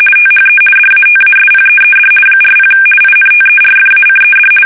RTTY, about 50 baud. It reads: THE DATASPEED 40 INTERFACED
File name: fsk850-50B / teletype-corp